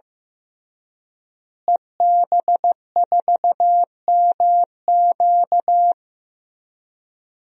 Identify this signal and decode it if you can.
Morse code, heard as a keyed tone, about 15 wpm, 700 Hz. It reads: EB4MQ
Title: DXCC-100 Call Signs - 5 Characters - Repeated 3x - 15wpm